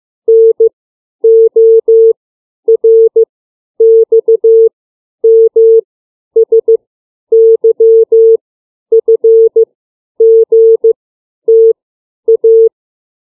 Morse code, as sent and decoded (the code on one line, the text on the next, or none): -. / --- / .-. / -..- / -- / ... / -.-- / ..-. / --. / - / .-
N O R X M S Y F G T A